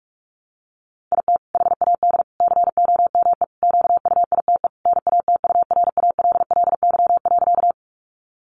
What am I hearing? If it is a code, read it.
Morse code, heard as a keyed tone, about 44 wpm, 700 Hz. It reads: IT HAD COME QUITE NATURALLY.
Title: Sentences from Top 1300 Words 45wpm